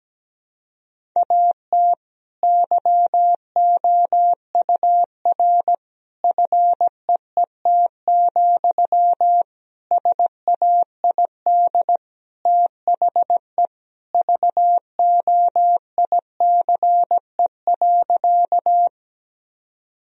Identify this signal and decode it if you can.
Morse code, heard as a keyed tone, about 17 wpm, 700 Hz. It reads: AT YOUR FEET, SAID THE VOICE.
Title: Sentences from Top 1000 Words - Review Part 1 - 17wpm